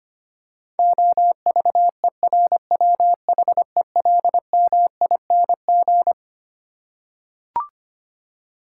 Morse code, heard as a keyed tone, about 25 wpm, 700 Hz. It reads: OVERWHELMING
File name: Twelve Letter Words - 25wpm